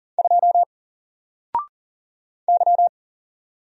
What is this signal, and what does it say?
Morse code, heard as a keyed tone, about 40 wpm, 700 Hz. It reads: 2 Y
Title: Single Letter-Number 40wpm